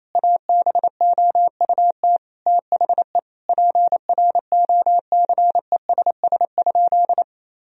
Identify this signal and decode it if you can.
Morse code, heard as a keyed tone, about 28 wpm, 700 Hz. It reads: ABOUT THE PROCESS?
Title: Sentences from Top 1000 Words - Review Part 3 - 28wpm